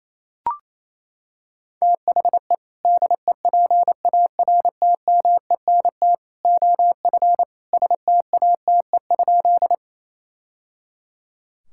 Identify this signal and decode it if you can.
Morse code, heard as a keyed tone, about 28 wpm, 700 Hz. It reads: THE DEPARTMENT OF STATE?
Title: Sentences from Top 1000 Words - Review Part 5 - 28wpm